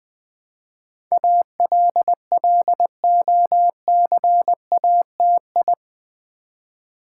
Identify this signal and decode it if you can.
Morse code, heard as a keyed tone, about 20 wpm, 700 Hz. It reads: ALLOCATI
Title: Build A Word - 4 Letters At A Time - 10-Letter Words - 20wpm